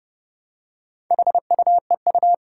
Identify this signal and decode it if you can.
Morse code, heard as a keyed tone, about 29 wpm, 700 Hz. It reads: HUEU
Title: Four Challenging Letter-Number Combinations 30wpm